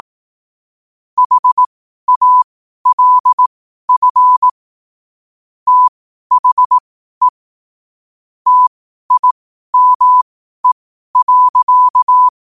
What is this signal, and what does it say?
Morse code, heard as a keyed tone, about 18 wpm, spaced as for 12 wpm, 1000 Hz. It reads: HALF THE TIME.
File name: qotdfiles / 12wpm